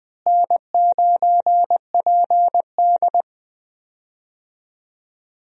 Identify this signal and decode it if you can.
Morse code, heard as a keyed tone, about 20 wpm, 700 Hz. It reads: N9PD
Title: USA Call Signs - 4 Characters - Repeated 3x - 20wpm